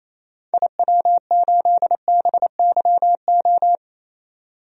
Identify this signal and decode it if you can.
Morse code, heard as a keyed tone, about 28 wpm, 700 Hz. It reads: IW8BYO
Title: DXCC-100 Call Signs - 6 Characters - 28wpm